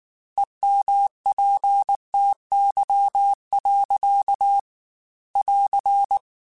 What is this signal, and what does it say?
Morse code, heard as a keyed tone, about 19 wpm, 800 Hz. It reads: EMPTY. <AR>